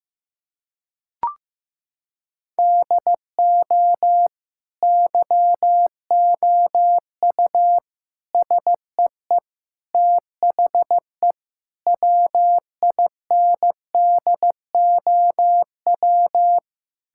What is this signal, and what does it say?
Morse code, heard as a keyed tone, about 15 wpm, 700 Hz. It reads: DO YOU SEE THE WINDOW
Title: Sentences from Top 1600 Words 15wpm